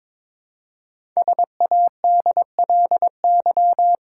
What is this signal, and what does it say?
Morse code, heard as a keyed tone, about 22 wpm, 700 Hz. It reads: SADLY
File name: 5 Letter Words - First 3 Letters Given - 22wpm